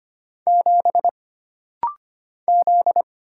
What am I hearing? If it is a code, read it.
Morse code, heard as a keyed tone, about 25 wpm, 700 Hz. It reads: 7 Z
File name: Single Character (Letter:Number:Punctuation:Prosign) - Rapid-Fire - 25wpm